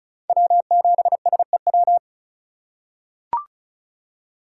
Morse code, heard as a keyed tone, about 36 wpm, 700 Hz. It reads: W7SEW
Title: USA Call Signs - 5 Characters - 3x Character Spacing - 35wpm